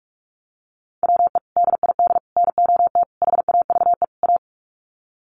Morse code, heard as a keyed tone, about 44 wpm, 700 Hz. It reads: WE DID NOT HAVE A
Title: Sets of 5 Words Encore2 45wpm